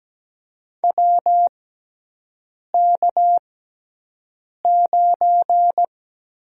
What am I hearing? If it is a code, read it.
Morse code, heard as a keyed tone, about 17 wpm, spaced as for 6 wpm, 700 Hz. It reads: WK9
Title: USA Call Signs - 4 Characters - 6x Character Spacing - 17wpm